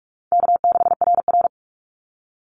Morse code, heard as a keyed tone, about 44 wpm, 700 Hz. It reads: K6RR
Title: Call Signs Repeated 2x - 45wpm